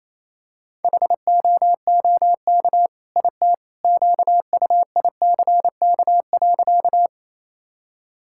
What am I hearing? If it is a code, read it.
Morse code, heard as a keyed tone, about 28 wpm, 700 Hz. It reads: HOOK IT QUICK.
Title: Sentences from Top 1600 Words 28wpm